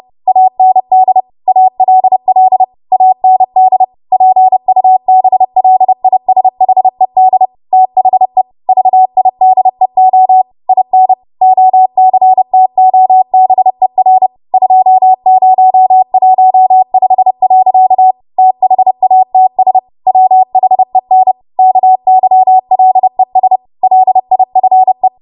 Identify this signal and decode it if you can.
Morse code, heard as a keyed tone, about 30 wpm, 750 Hz. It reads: AND ALL AND PUBLISHED THE VIDEO IN OCTOBER 2015. THATS WHEN KYLES LIFE
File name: Morse / 30WPM